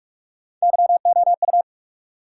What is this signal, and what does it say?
Morse code, heard as a keyed tone, about 60 wpm, spaced as for 52 wpm, 700 Hz. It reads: YOU
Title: Speed-Racing Top 180 QSO Elements 45wpm